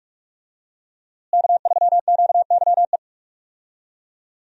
Morse code, heard as a keyed tone, about 41 wpm, 700 Hz. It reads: K3QYE
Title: USA Call Signs - 5 Characters - Repeated 3x - 45wpm